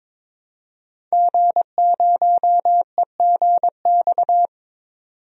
Morse code, heard as a keyed tone, about 22 wpm, 700 Hz. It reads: G0EGX
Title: DXCC-100 Call Signs - 5 Characters - 22wpm